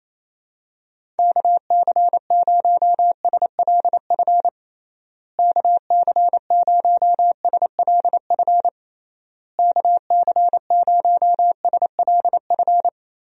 Morse code, heard as a keyed tone, about 28 wpm, 700 Hz. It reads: KC0SLF KC0SLF KC0SLF
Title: USA Call Signs - 6 Characters - Repeated 3x - 28wpm